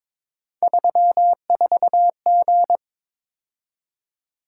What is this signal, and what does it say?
Morse code, heard as a keyed tone, about 26 wpm, 700 Hz. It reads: 34G
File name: Speed-Racing - Three Letter-Number Combinations - 22wpm